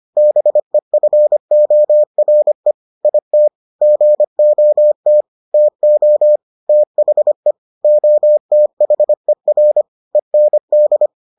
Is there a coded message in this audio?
Yes: Morse code from 0.2 to 11.1 s, heard as a keyed tone, about 25 wpm, 600 Hz. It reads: BEFORE IT GOT TO THE OTHER END